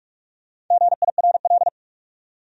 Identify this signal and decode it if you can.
Morse code, heard as a keyed tone, about 40 wpm, 700 Hz. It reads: GIRL